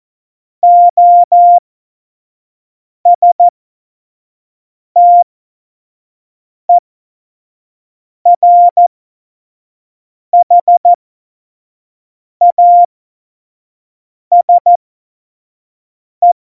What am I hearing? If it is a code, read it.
Morse code, heard as a keyed tone, about 14 wpm, spaced as for 5 wpm, 700 Hz. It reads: OSTERHASE
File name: Morsekurs / Witze / Witz 14 April 2020, 25 F70